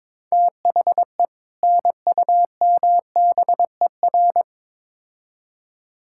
Morse code, heard as a keyed tone, about 22 wpm, 700 Hz. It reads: THE NUMBER
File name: Build A Phrase - 2-Word Phrases - 22wpm